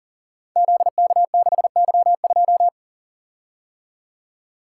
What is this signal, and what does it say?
Morse code, heard as a keyed tone, about 40 wpm, 700 Hz. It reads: ZK6Y2
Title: Code Groups - 5 Letter:Numbers - 40wpm